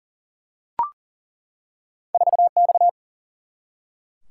Morse code, heard as a keyed tone, about 40 wpm, 700 Hz. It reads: 4X
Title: Single-Letter Single-Number 40wpm